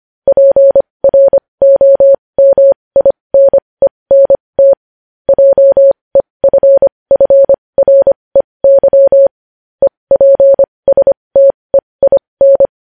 Morse code, heard as a keyed tone, about 25 wpm, spaced as for 20 wpm, 550 Hz. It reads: PROMINENT JEFFREY EPSTEIN
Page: news headlines in morse code at 20 wpm